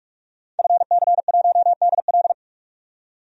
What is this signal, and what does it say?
Morse code, heard as a keyed tone, about 46 wpm, 700 Hz. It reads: FC1DL
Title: DXCC-100 Call Signs - 5 Characters - 45wpm